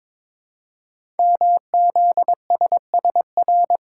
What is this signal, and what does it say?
Morse code, heard as a keyed tone, about 22 wpm, 700 Hz. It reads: MZSSR
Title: Code Groups - 5 Letters - 22wpm